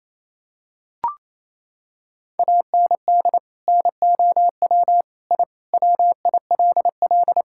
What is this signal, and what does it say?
Morse code, heard as a keyed tone, about 27 wpm, 700 Hz. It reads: AND NOW I WILL
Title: Sentences from Top 1000 Words - Review Part 6 - 28wpm